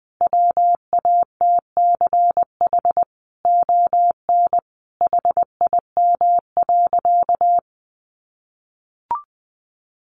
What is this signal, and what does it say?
Morse code, heard as a keyed tone, about 20 wpm, 700 Hz. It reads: WATCH ON HIM.
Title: Sentences from Top 1300 Words 20wpm